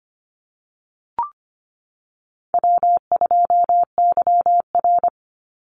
Morse code, heard as a keyed tone, about 25 wpm, 700 Hz. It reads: W2YR
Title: Call Signs - 25wpm